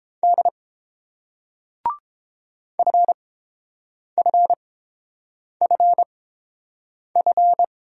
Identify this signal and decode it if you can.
Morse code, heard as a keyed tone, about 27 wpm, spaced as for 7 wpm, 700 Hz. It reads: D FFFF
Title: Speed-Racing Single Letters 22wpm